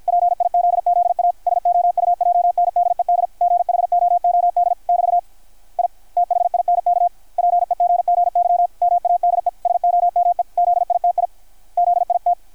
Morse code, heard as a keyed tone, about 53 wpm, 700 Hz, with radio noise. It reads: PIZZA SOUJNDED GHOOD <BT> I THINK PEGGY MADE SOME ZITI ZIT
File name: archived / CRSnet